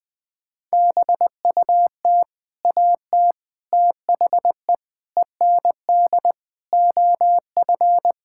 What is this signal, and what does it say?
Morse code, heard as a keyed tone, about 20 wpm, 700 Hz. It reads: BUT AT THE END OF
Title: Sets of 8 Words - 20wpm